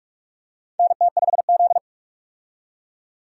Morse code, heard as a keyed tone, about 46 wpm, 700 Hz. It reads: NT5Z
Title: USA Call Signs - 4 Characters - 45wpm